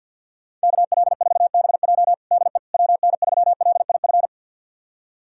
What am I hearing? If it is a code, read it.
Morse code, heard as a keyed tone, about 52 wpm, 700 Hz. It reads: KL4BJ DE WN3LIF
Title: USA Callsign DE Callsign 50wpm